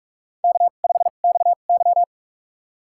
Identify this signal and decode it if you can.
Morse code, heard as a keyed tone, about 41 wpm, spaced as for 31 wpm, 700 Hz. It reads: K5XY
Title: USA Call Signs - 4 Characters - 2x Character Spacing - 45wpm